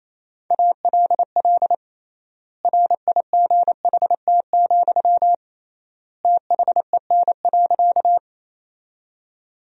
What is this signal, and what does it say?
Morse code, heard as a keyed tone, about 27 wpm, 700 Hz. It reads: ALL RIGHT, THEN.
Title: 3x Word-Spacing - Sentences from Top 200 Words - 28wpm